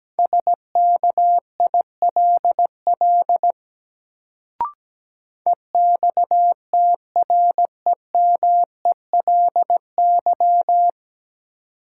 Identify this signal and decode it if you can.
Morse code, heard as a keyed tone, about 17 wpm, 700 Hz. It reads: SKILL EXTREMELY
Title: Sentences from Top 1200 Words - New Word Drill Alt - 17wpm